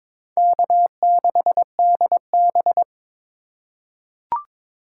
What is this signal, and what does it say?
Morse code, heard as a keyed tone, about 22 wpm, 700 Hz. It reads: K6DB
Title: USA Call Signs - 4 Characters - 3x Character Spacing - 22wpm